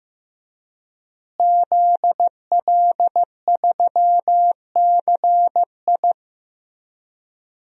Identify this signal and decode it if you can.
Morse code, heard as a keyed tone, about 16 wpm, 700 Hz. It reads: ZL3CI
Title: DXCC-100 Call Signs - 5 Characters - Repeated 2x - 15wpm